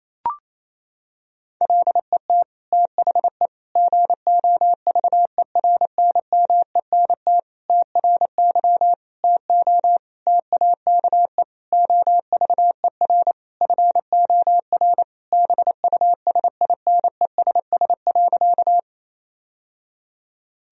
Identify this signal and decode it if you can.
Morse code, heard as a keyed tone, about 28 wpm, 700 Hz. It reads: LET THE GOVERNMENT TRY TO TAKE OVER FOR BUSINESS.
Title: Sentences from Top 1000 Words - Review Part 3 - 28wpm